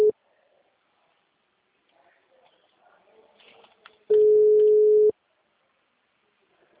Ringback tone (Europe, 425 Hz).
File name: out- -2020- - - .31558